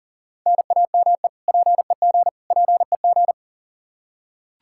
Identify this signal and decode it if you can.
Morse code, heard as a keyed tone, about 40 wpm, 700 Hz. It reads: NAME PEG PEG
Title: Practice QSO - Exchange 1 - Beginning - 40wpm